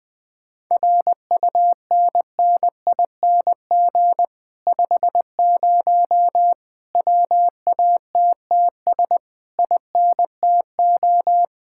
Morse code, heard as a keyed tone, about 21 wpm, 700 Hz. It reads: RUNNING 50 WATTS INTO
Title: Speed-Racing Common QSO Phrases - 20wpm